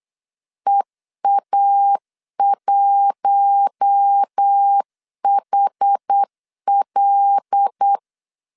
Morse code, heard as a keyed tone, about 8 wpm, 800 Hz. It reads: EA1HL
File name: LEZIONI / 8 YJ45 / 8 YJ4 8wpm 3246